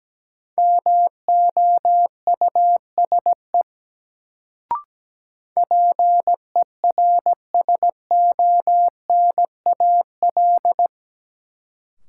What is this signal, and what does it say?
Morse code, heard as a keyed tone, about 17 wpm, 700 Hz. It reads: MOUSE PERSONAL
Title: Sentences from Top 700 Words - New Word Drill - 17wpm